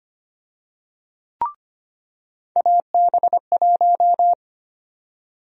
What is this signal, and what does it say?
Morse code, heard as a keyed tone, about 25 wpm, 700 Hz. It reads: AB1
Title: USA Call Sign Prefixes - 25wpm